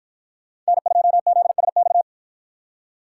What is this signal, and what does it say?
Morse code, heard as a keyed tone, about 51 wpm, 700 Hz. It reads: N2ZSX